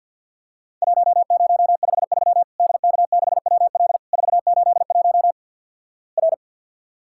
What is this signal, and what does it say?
Morse code, heard as a keyed tone, about 47 wpm, 700 Hz. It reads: 1053 DK6WL 481 R
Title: Worked All Europe DX Contest - QTC Exchange with Repeats - 50wpm